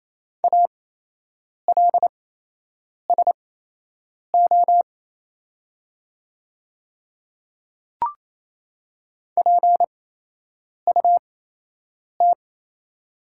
Morse code, heard as a keyed tone, about 27 wpm, spaced as for 8 wpm, 690 Hz. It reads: ALSO PUT
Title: 8x Character Spacing - Top 200 Words - 28wpm